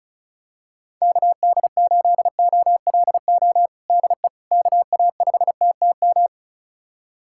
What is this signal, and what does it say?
Morse code, heard as a keyed tone, about 36 wpm, 700 Hz. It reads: KD8OLO DE KA5TTM
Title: USA Callsign DE Callsign 35wpm